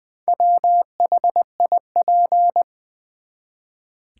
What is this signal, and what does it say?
Morse code, heard as a keyed tone, about 20 wpm, 700 Hz. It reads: WHIP
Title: Build A Word - 1 Letter At A Time - 4-Letter Words - 20wpm